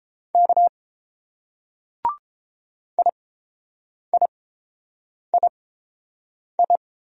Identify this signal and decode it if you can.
Morse code, heard as a keyed tone, about 29 wpm, spaced as for 7 wpm, 700 Hz. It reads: K IIII